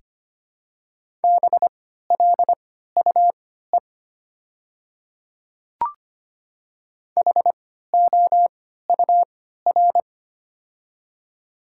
Morse code, heard as a keyed tone, about 25 wpm, spaced as for 14 wpm, 700 Hz. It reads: BLUE HOUR
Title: 3x Character Spacing - Top 500 Words - 25wpm